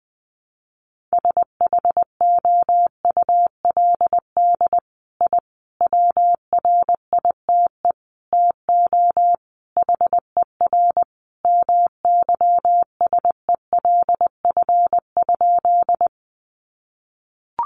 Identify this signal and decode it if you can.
Morse code, heard as a keyed tone, about 20 wpm, 700 Hz. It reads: SHOULD I WRITE TO HER MYSELF?
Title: Sentences from Top 1300 Words 20wpm